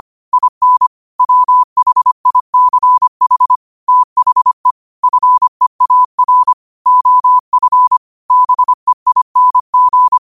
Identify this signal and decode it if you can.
Morse code, heard as a keyed tone, about 25 wpm, 1000 Hz. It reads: IN WHICH THE FEAR OF BEING